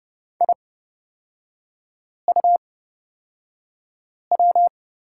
Morse code, heard as a keyed tone, about 30 wpm, spaced as for 5 wpm, 690 Hz. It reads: IUW
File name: Lesson 37e - Letter x - Three Characters 30wpm